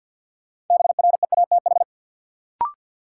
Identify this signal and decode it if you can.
Morse code, heard as a keyed tone, about 53 wpm, 700 Hz. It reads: BREATH
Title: Sets of 5 Words Encore - New Word Drill - 50wpm